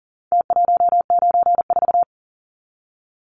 Morse code, heard as a keyed tone, about 40 wpm, 700 Hz. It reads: T194
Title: Numbers - Contest Style - 4-Digit 40wpm